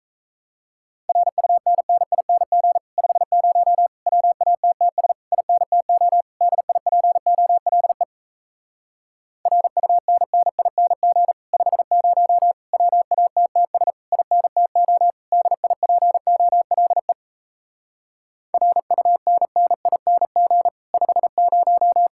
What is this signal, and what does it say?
Morse code, heard as a keyed tone, about 37 wpm, 700 Hz. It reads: RUNNING 50 WATTS INTO DIPOLE RUNNING 50 WATTS INTO DIPOLE RUNNING 50